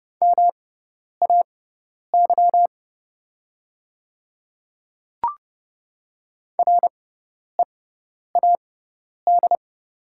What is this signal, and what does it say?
Morse code, heard as a keyed tone, about 29 wpm, spaced as for 10 wpm, 700 Hz. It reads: MAY READ
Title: 6x Character Spacing - Top 300 Words - 30wpm